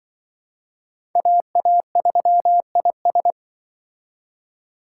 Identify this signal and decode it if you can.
Morse code, heard as a keyed tone, about 24 wpm, 700 Hz. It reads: AA3IS
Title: Speed Racing - USA 2x2 Call Signs 20wpm